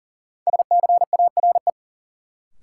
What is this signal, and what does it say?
Morse code, heard as a keyed tone, about 40 wpm, 700 Hz. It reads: SCARE
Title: Sentences from Top 1000 Words - Review Part 5 - New Word Drill - 40wpm